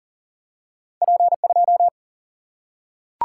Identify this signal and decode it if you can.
Morse code, heard as a keyed tone, about 38 wpm, 700 Hz. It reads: P 2
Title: Single-Letter Single-Number 40wpm